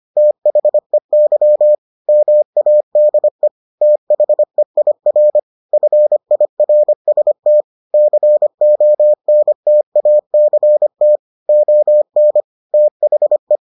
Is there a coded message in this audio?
Yes: Morse code from 0.2 to 13.6 s, heard as a keyed tone, about 25 wpm, 600 Hz. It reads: THEY MADE THEIR FIRST CONTACT ON THE